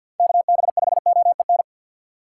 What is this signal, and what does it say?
Morse code, heard as a keyed tone, about 48 wpm, 700 Hz. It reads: KB5OEN
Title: USA Call Signs - 6 Characters - 4x Character Spacing - 50wpm